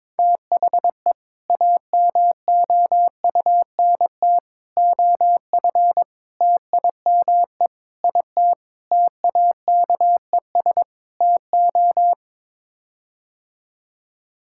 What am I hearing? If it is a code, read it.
Morse code, heard as a keyed tone, about 22 wpm, 700 Hz. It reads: THE AMOUNT OF TIME IT TAKES TO
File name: practice / 22wpm / Sets of 7 Words - 22wpm